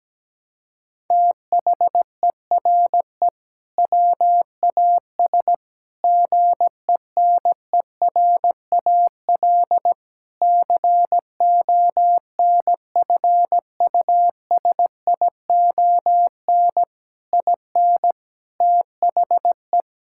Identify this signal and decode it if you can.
Morse code, heard as a keyed tone, about 17 wpm, 700 Hz. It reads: THERE WAS GENERAL CONFUSION IN THE
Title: Sentences from Top 1600 Words 17wpm